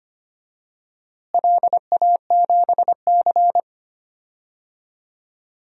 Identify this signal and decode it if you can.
Morse code, heard as a keyed tone, about 25 wpm, 700 Hz. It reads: LA7C
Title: DXCC-100 Call Signs - 4 Characters - Repeated 2x - 25wpm